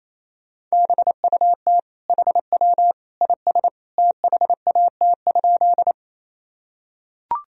Morse code, heard as a keyed tone, about 28 wpm, 700 Hz. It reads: BUT HW IS THAT?
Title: Sentences from Top 100 Words and Abbreviations 28wpm